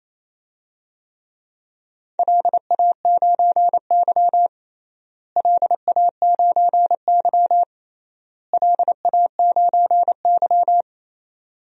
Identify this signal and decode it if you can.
Morse code, heard as a keyed tone, about 28 wpm, 700 Hz. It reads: LA9Y LA9Y LA9Y
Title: DXCC-100 Call Signs - 4 Characters - Repeated 3x - 28wpm